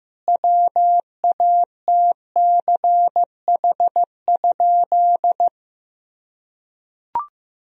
Morse code, heard as a keyed tone, about 16 wpm, 700 Hz. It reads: WATCH?
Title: Sentences from Top 1000 Words - Review Part 3 - 15wpm